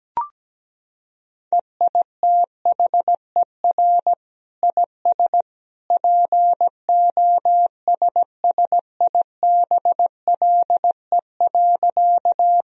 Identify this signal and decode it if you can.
Morse code, heard as a keyed tone, about 17 wpm, 700 Hz. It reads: EITHER IS POSSIBLE.
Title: Sentences from Top 1000 Words - Review Part 4 - 17wpm